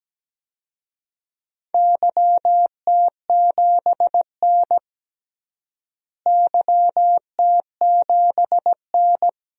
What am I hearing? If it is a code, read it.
Morse code, heard as a keyed tone, about 17 wpm, 700 Hz. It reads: YT7N YT7N
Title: DXCC-100 Call Signs - 4 Characters - Repeated 3x - 17wpm